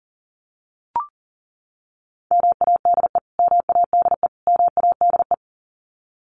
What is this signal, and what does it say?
Morse code, heard as a keyed tone, about 40 wpm, 700 Hz. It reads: MADE MADE MADE
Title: Top 500 Words Repeated 40wpm